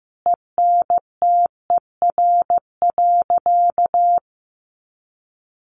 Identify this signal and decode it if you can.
Morse code, heard as a keyed tone, about 15 wpm, 700 Hz. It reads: ENTER.
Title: Sentences from Top 1300 Words 15wpm